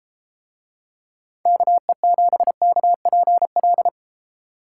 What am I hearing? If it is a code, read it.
Morse code, heard as a keyed tone, about 31 wpm, 700 Hz. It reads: KE7KPL